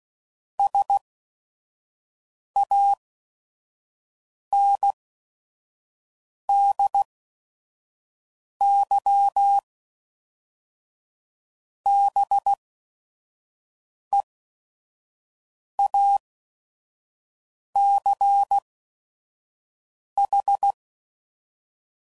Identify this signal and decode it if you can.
Morse code, heard as a keyed tone, about 16 wpm, spaced as for 5 wpm, 800 Hz. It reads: SANDYBEACH
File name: CW Code Groups @5WPM / fromSM's / Cities